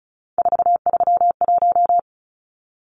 Morse code, heard as a keyed tone, about 34 wpm, 700 Hz. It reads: 431
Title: Numbers - 3-Digit 35wpm